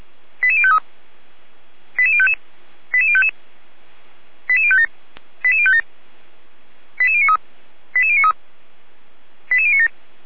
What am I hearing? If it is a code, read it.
Five-tone selective call (ZVEI), heard as quick runs of five tones, tones of about 70 ms. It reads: 88053 88055 88055 88057 88057 88093 88093 88097
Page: /455/SDR/ZVEI/88019/2022.05.30/